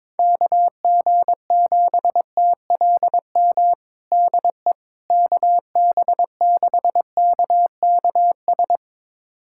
Morse code, heard as a keyed tone, about 22 wpm, 700 Hz. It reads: KG7TLM DE KB6KKS
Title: USA Callsign DE Callsign - 6 Characters - 22wpm